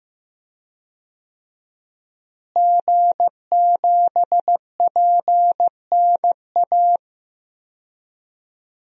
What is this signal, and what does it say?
Morse code, heard as a keyed tone, about 15 wpm, 700 Hz. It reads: G7PNA
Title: DXCC-100 Call Signs - 5 Characters - Repeated 3x - 15wpm